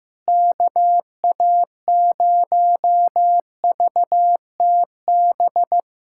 Morse code, heard as a keyed tone, about 15 wpm, 700 Hz. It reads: KA0VTB
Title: USA Callsign DE Callsign 15wpm